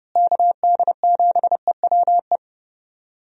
Morse code, heard as a keyed tone, about 30 wpm, 700 Hz. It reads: KD7EWE
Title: USA Call Signs - 6 Characters - Repeated 3x - 30wpm